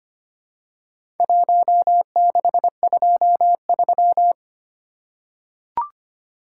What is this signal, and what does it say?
Morse code, heard as a keyed tone, about 25 wpm, 700 Hz. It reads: 1623
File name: practice / 25wpm / Four Letter-Number Combinations - 25wpm